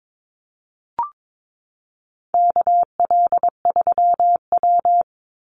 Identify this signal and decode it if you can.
Morse code, heard as a keyed tone, about 22 wpm, 700 Hz. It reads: KL3W
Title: Call Signs Repeated 2x - 22wpm